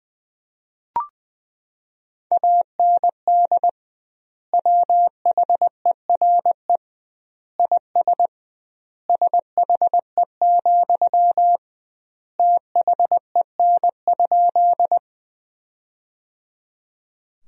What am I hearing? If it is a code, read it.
Morse code, heard as a keyed tone, about 20 wpm, 700 Hz. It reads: AND WHERE IS SHE, THEN?
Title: 2x Word-Spacing - Sentences from Top 200 Words - 20wpm